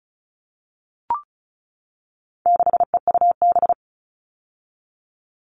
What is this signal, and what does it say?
Morse code, heard as a keyed tone, about 34 wpm, 700 Hz. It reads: 6EUB
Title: All Cut Numbers - 4-Digit 35wpm